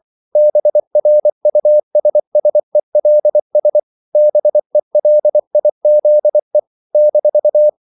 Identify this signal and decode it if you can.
Morse code, heard as a keyed tone, about 24 wpm, 600 Hz. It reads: BRUSSELS BELIZE -
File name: CW / Practice / NationalCapitals24